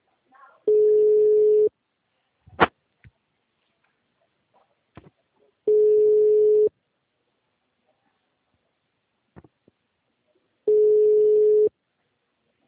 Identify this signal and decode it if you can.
Ringback tone (Europe, 425 Hz).